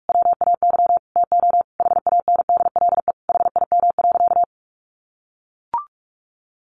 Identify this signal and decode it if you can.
Morse code, heard as a keyed tone, about 44 wpm, 700 Hz. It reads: WAY TO HANDLE HIM.
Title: Sentences from Top 1100 Words 45wpm